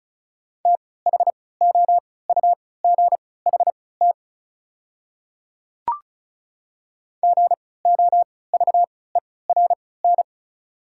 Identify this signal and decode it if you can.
Morse code, heard as a keyed tone, about 34 wpm, spaced as for 20 wpm, 690 Hz. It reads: THOUGHT GOVERN
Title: 3x Character Spacing - Top 500 Words - 35wpm